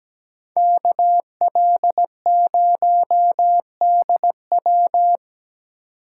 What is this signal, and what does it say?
Morse code, heard as a keyed tone, about 20 wpm, 700 Hz. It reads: KL0DW
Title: Speed Racing - USA Call Signs 17wpm